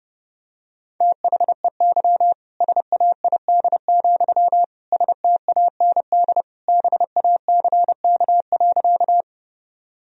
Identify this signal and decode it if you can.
Morse code, heard as a keyed tone, about 30 wpm, 700 Hz. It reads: THEY SAID, STAND BACK.
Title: Sentences from Top 1000 Words - Review Part 1 - 30wpm